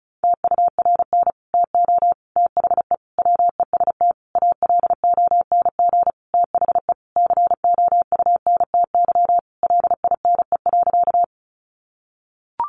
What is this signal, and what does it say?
Morse code, heard as a keyed tone, about 34 wpm, 700 Hz. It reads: TURN TO THE WEST ALONG THE COUNTY LINE.
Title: Sentences from Top 1200 Words 35wpm